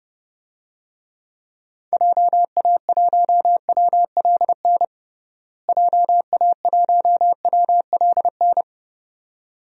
Morse code, heard as a keyed tone, about 30 wpm, 700 Hz. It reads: JA1WLN JA1WLN
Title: DXCC-100 Call Signs - 6 Characters - Repeated 2x - 30wpm